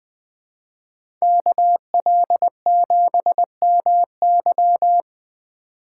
Morse code, heard as a keyed tone, about 20 wpm, 700 Hz. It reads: KL7MY
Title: USA Call Signs - 5 Characters - 4x Character Spacing - 20wpm